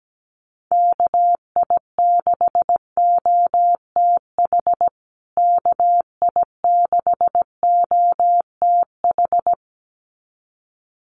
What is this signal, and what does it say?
Morse code, heard as a keyed tone, about 17 wpm, 700 Hz. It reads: KI6OTH KI6OTH
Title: USA Call Signs Repeated 2x 17wpm